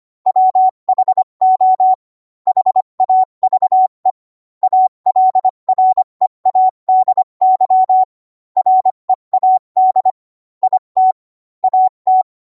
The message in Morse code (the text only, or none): WHO HAVE ALREADY READ IT AT